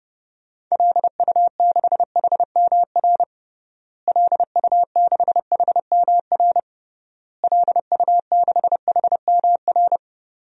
Morse code, heard as a keyed tone, about 31 wpm, 700 Hz. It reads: LU6HMR LU6HMR LU6HMR
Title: DXCC-100 Call Signs - 6 Characters - Repeated 3x - 30wpm